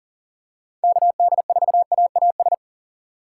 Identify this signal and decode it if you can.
Morse code, heard as a keyed tone, about 40 wpm, 700 Hz. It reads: KD4AAS
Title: USA Call Signs - 6 Characters - 6x Character Spacing - 40wpm